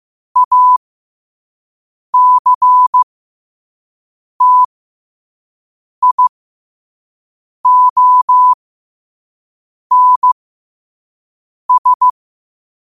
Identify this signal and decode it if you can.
Morse code, heard as a keyed tone, about 15 wpm, spaced as for 5 wpm, 1000 Hz. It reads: ACTIONS
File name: qotdfiles / 5wpm